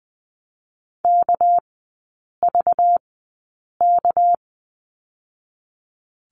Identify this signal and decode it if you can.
Morse code, heard as a keyed tone, about 20 wpm, 700 Hz. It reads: K V K
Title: 198 Lesson 32e - Letter k - Three Characters 20wpm